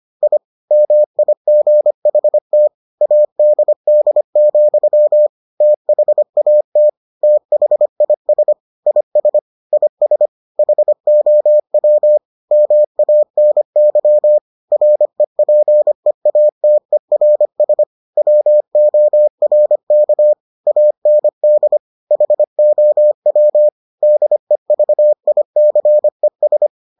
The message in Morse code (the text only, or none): I MIGHT ADD, THAT THIS IS IS HOW MANY REPEATERS WORK AND HOW DEVICES